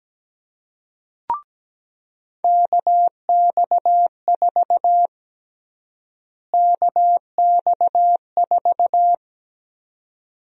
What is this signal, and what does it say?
Morse code, heard as a keyed tone, about 17 wpm, 700 Hz. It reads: KX4 KX4
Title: USA Call Sign Prefixes - Repeated 2x - 17wpm